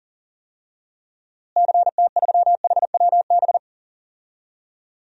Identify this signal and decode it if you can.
Morse code, heard as a keyed tone, about 40 wpm, 700 Hz. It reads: CT3HWB
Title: DXCC-100 Call Signs - 6 Characters - Repeated 3x - 40wpm